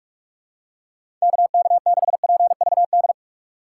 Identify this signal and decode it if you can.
Morse code, heard as a keyed tone, about 43 wpm, 700 Hz. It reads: KK6PVD
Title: USA Call Signs - 6 Characters - 6x Character Spacing - 45wpm